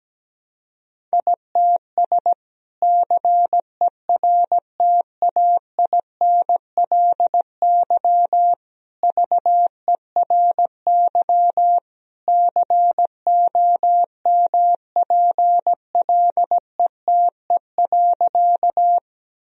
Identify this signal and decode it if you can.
Morse code, heard as a keyed tone, about 17 wpm, 700 Hz. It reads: ITS CERTAINLY VERY COMPLETE.
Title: Sentences from Top 1000 Words - Review Part 2 - 17wpm